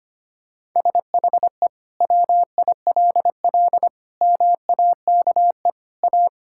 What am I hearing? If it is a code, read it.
Morse code, heard as a keyed tone, about 25 wpm, 700 Hz. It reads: SHE WILL MAKE A
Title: Sentences from Top 1000 Words - Review Part 6 - 25wpm